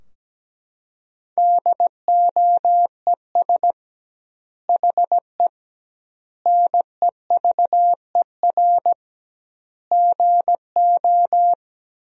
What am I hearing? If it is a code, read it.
Morse code, heard as a keyed tone, about 17 wpm, 700 Hz. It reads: DOES HE NEVER GO
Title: 2x Word-Spacing - Sentences from Top 300 Words - 17wpm